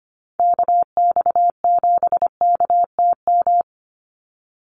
Morse code, heard as a keyed tone, about 25 wpm, 700 Hz. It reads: KX7KTM
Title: USA Call Signs 25wpm